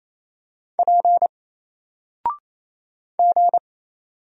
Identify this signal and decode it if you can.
Morse code, heard as a keyed tone, about 27 wpm, 700 Hz. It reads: P G
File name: Single-Letter-Number-28wpm